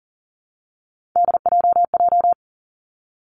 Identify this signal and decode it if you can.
Morse code, heard as a keyed tone, about 42 wpm, 700 Hz. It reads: DJJ
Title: Three Challenging Letter-Number Combinations 40wpm